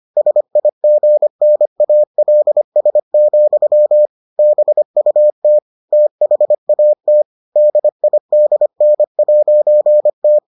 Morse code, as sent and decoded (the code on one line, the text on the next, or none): ... .. --. -. .- .-.. ... --..-- / -... ..- - / - .... .- - / -.. .. -.. -. .----. -
SIGNALS, BUT THAT DIDN'T